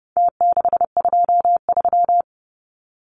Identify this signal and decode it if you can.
Morse code, heard as a keyed tone, about 30 wpm, 700 Hz. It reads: T623
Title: Numbers - Contest Style - 4-Digit 30wpm